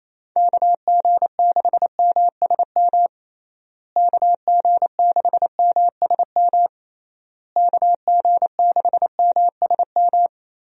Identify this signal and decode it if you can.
Morse code, heard as a keyed tone, about 28 wpm, 700 Hz. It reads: KG6MSM KG6MSM KG6MSM
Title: USA Call Signs - 6 Characters - Repeated 3x - 28wpm